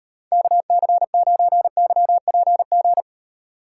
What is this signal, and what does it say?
Morse code, heard as a keyed tone, about 31 wpm, 700 Hz. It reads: KC9YPG
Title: Speed Racing - USA 2x3 Call Signs 25wpm